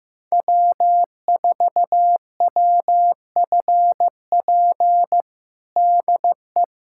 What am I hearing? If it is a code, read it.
Morse code, heard as a keyed tone, about 16 wpm, 700 Hz. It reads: W4WFP DE
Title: USA Callsign DE Callsign - 5 Characters - 15wpm